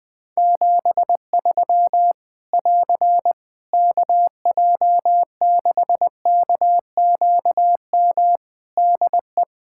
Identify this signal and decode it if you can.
Morse code, heard as a keyed tone, about 20 wpm, 700 Hz. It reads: 73 <AR> KJ6KQM DE
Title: Practice QSO - Exchange 4 - Ending - 20wpm